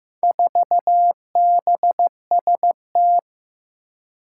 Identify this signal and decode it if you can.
Morse code, heard as a keyed tone, about 15 wpm, 700 Hz. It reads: 4BST
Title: Four Challenging Letter-Number Combinations 15wpm